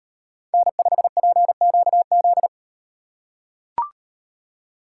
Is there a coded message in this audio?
Yes: Morse code from 0.5 to 2.5 s, heard as a keyed tone, about 38 wpm, 700 Hz. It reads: N5PQZ